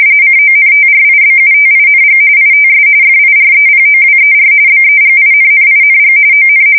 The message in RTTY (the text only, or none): CONTRIBUTING ITS FIREPOWER BEGINNING ON